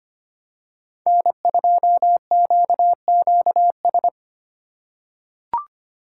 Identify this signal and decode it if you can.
Morse code, heard as a keyed tone, about 25 wpm, 700 Hz. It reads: N2QQS
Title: USA Call Signs - 5 Characters - 2x Character Spacing - 25wpm